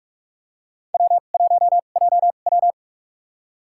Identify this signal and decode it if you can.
Morse code, heard as a keyed tone, about 46 wpm, spaced as for 32 wpm, 700 Hz. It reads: W1JW
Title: USA Call Signs - 4 Characters - 2x Character Spacing - 45wpm